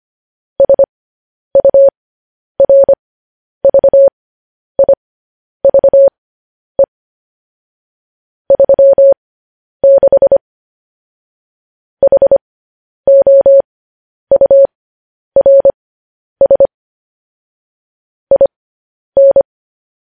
Morse code, heard as a keyed tone, about 25 wpm, spaced as for 10 wpm, 550 Hz. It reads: SURVIVE 36 HOURS IN
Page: news-headlines-in-morse-code-at-10-wpm